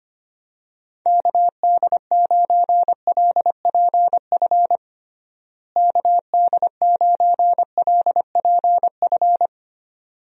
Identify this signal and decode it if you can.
Morse code, heard as a keyed tone, about 25 wpm, 700 Hz. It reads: KD9LPF KD9LPF
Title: USA Call Signs - 6 Characters - Repeated 2x - 25wpm